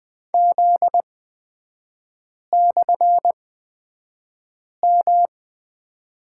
Morse code, heard as a keyed tone, about 20 wpm, spaced as for 5 wpm, 700 Hz. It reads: Z/M
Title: 255 Lesson 41e - Period - Three Characters 20wpm